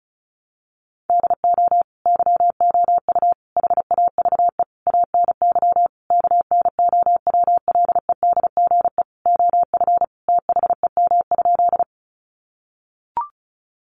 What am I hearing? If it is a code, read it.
Morse code, heard as a keyed tone, about 34 wpm, 700 Hz. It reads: DO YOU HAVE ANY KNOWLEDGE OF THEM?
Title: Sentences from Top 600 Words 35wpm